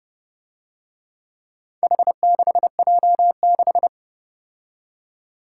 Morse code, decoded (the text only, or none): H6J6